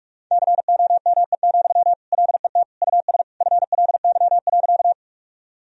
Morse code, heard as a keyed tone, about 44 wpm, 700 Hz. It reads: COME, LET US FLY.
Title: Sentences from Top 1000 Words - Review Part 2 - 45wpm